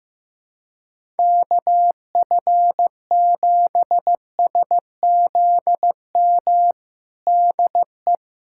Morse code, heard as a keyed tone, about 15 wpm, 700 Hz. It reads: KF7SZM DE
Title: USA Callsign DE Callsign - 6 Characters - 15wpm